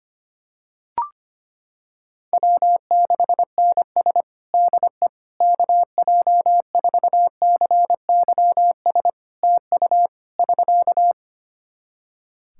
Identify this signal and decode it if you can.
Morse code, heard as a keyed tone, about 25 wpm, 700 Hz. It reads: W6NS DE KJ4CYS TU <SK>
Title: Practice QSO - Exchange 4 - Ending - 25wpm